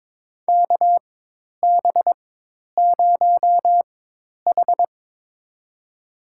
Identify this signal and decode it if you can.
Morse code, heard as a keyed tone, about 22 wpm, spaced as for 10 wpm, 700 Hz. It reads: KB0H
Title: USA Call Signs - 4 Characters - 4x Character Spacing - 22wpm